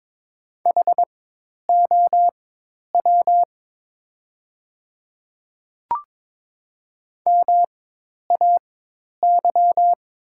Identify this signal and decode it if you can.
Morse code, heard as a keyed tone, about 21 wpm, spaced as for 10 wpm, 700 Hz. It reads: HOW MAY
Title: 4x Character Spacing - Top 100 Words - 22wpm